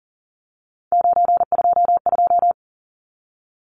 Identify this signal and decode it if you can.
Morse code, heard as a keyed tone, about 40 wpm, 700 Hz. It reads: 922
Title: Numbers - 3-Digit 40wpm